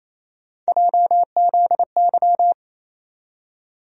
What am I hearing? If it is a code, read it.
Morse code, heard as a keyed tone, about 28 wpm, 700 Hz. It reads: JZY
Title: USA Call Sign Postfixes - 3 Letters 28wpm